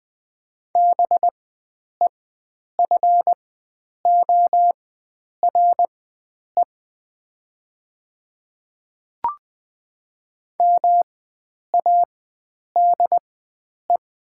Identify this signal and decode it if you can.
Morse code, heard as a keyed tone, about 20 wpm, spaced as for 9 wpm, 700 Hz. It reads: BEFORE MADE
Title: 4x Character Spacing - Top 300 Words - 20wpm